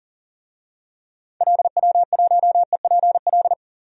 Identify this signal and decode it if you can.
Morse code, heard as a keyed tone, about 40 wpm, 700 Hz. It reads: LW1EPL
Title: DXCC-100 Call Signs - 6 Characters - 40wpm